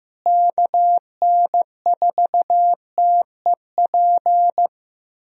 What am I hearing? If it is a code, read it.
Morse code, heard as a keyed tone, about 16 wpm, 700 Hz. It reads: KN4TEP
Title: USA Call Signs - 6 Characters - Repeated 3x - 15wpm